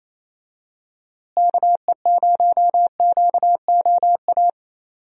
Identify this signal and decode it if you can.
Morse code, heard as a keyed tone, about 28 wpm, 700 Hz. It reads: KE0QOA
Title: USA Call Signs - 6 Characters - Repeated 3x - 28wpm